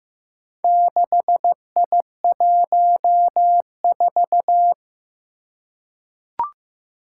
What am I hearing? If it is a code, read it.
Morse code, heard as a keyed tone, about 15 wpm, 700 Hz. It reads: 6I14
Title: Four Challenging Letter-Number Combinations 15wpm